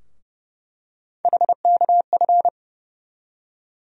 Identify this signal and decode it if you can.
Morse code, heard as a keyed tone, about 30 wpm, 700 Hz. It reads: HKF
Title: USA Call Sign Postfixes - 3 Letters 30wpm